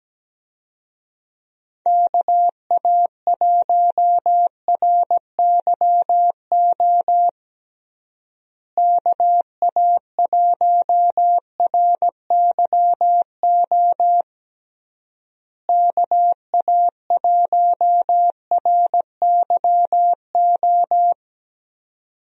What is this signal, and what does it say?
Morse code, heard as a keyed tone, about 17 wpm, 700 Hz. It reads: KA1RYO KA1RYO KA1RYO
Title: USA Call Signs - 6 Characters - Repeated 3x - 17wpm